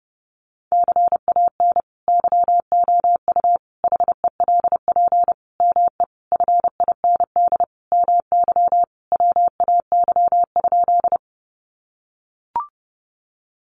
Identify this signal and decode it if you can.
Morse code, heard as a keyed tone, about 30 wpm, 700 Hz. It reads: CAN YOU HELP ME FIND MY WAY?
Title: Sentences from Top 200 Words 30wpm